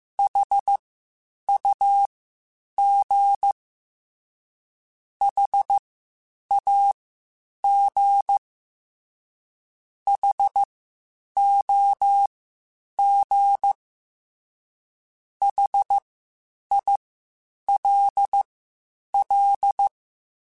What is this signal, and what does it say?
Morse code, heard as a keyed tone, about 15 wpm, spaced as for 8 wpm, 800 Hz. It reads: HUG HAG HOG HILL
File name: H Prefixes / H mp3s